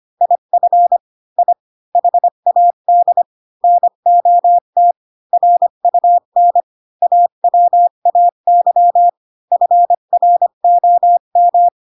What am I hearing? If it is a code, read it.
Morse code, heard as a keyed tone, about 25 wpm, 700 Hz. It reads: IF I HAD NOT RUN AWAY FROM
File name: Pinocchio / Pinocchio ch5 20 25 ik1hge